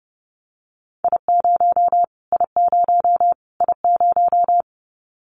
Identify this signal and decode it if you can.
Morse code, heard as a keyed tone, about 30 wpm, 700 Hz. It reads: I0 I0 I0
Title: Callsign Prefixes for Top 25 DXCC - Repeated 3x - 30wpm